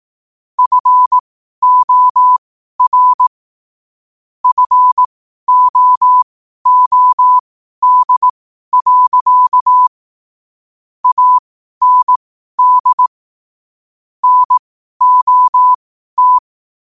Morse code, heard as a keyed tone, about 18 wpm, spaced as for 12 wpm, 1000 Hz. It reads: FOR FOOD. AND NOT
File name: qotdfiles / 12wpm